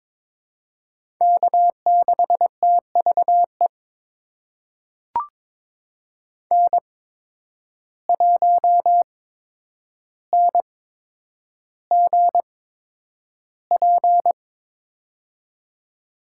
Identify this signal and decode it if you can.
Morse code, heard as a keyed tone, about 23 wpm, spaced as for 10 wpm, 700 Hz. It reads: K6TVE N1NGP
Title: USA Call Signs - 5 Characters - 8x Character Spacing - 22wpm